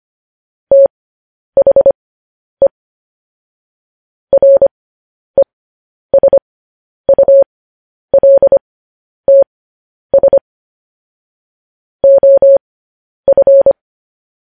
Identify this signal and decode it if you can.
Morse code, heard as a keyed tone, about 25 wpm, spaced as for 10 wpm, 550 Hz. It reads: THE RESULTS OF
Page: news-headlines-in-morse-code-at-10-wpm